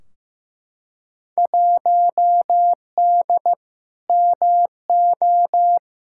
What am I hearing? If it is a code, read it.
Morse code, heard as a keyed tone, about 15 wpm, 700 Hz. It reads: 1D MO
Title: Field Day Exchanges - 15wpm